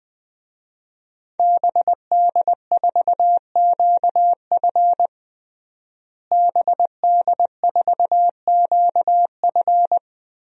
Morse code, heard as a keyed tone, about 20 wpm, 700 Hz. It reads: BD4QF BD4QF
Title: DXCC-100 Call Signs - 5 Characters - Repeated 3x - 20wpm